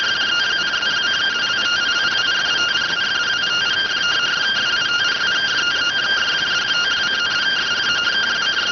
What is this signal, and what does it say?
Two-tone teleprinter signal, about 170 Hz shift.